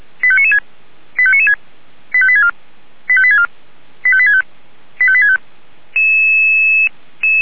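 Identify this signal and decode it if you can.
Five-tone selective call (ZVEI), heard as quick runs of five tones, tones of about 70 ms. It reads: 85996 85996 85864 85864 85865 85865
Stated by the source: ZVEI five-tone call received by SDR.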